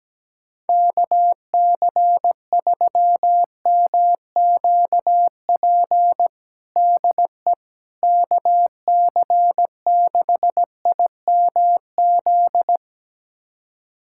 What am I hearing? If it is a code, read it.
Morse code, heard as a keyed tone, about 17 wpm, 700 Hz. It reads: KC3MQP DE KC6IMZ
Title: USA Callsign DE Callsign 17wpm